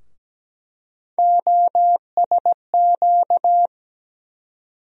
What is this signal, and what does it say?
Morse code, heard as a keyed tone, about 17 wpm, 700 Hz. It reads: OSQ
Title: USA Call Sign Postfixes - 3 Letters 17wpm